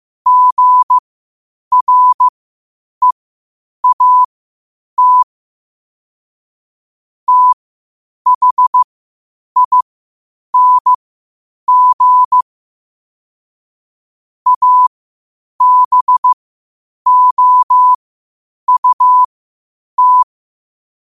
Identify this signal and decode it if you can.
Morse code, heard as a keyed tone, about 15 wpm, spaced as for 8 wpm, 1000 Hz. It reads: GREAT THING ABOUT
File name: qotdfiles / 8wpm